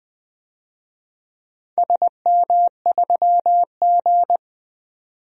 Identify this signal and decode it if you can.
Morse code, heard as a keyed tone, about 20 wpm, 700 Hz. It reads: SM3G
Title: DXCC-100 Call Signs - 4 Characters - 20wpm